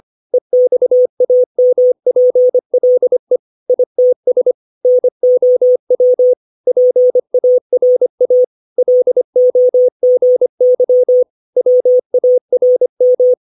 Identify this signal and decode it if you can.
Morse code, heard as a keyed tone, about 25 wpm, 500 Hz. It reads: EXAMPLE ITS NOW PARA LOGY WARM
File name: 632 words / 632 Words 25WPM